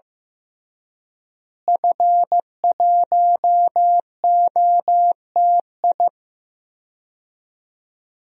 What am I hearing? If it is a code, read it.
Morse code, heard as a keyed tone, about 15 wpm, 700 Hz. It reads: F1OTI
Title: DXCC-100 Call Signs - 5 Characters - Repeated 2x - 15wpm